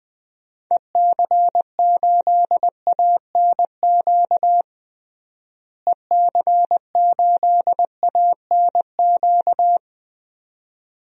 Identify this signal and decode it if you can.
Morse code, heard as a keyed tone, about 20 wpm, 700 Hz. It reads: EC8ANQ EC8ANQ
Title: DXCC-100 Call Signs - 6 Characters - Repeated 2x - 20wpm